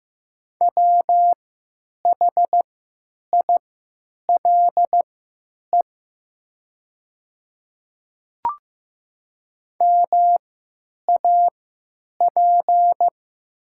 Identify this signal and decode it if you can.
Morse code, heard as a keyed tone, about 15 wpm, spaced as for 9 wpm, 700 Hz. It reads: WHILE MAP
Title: 3x Character Spacing - Top 500 Words - 15wpm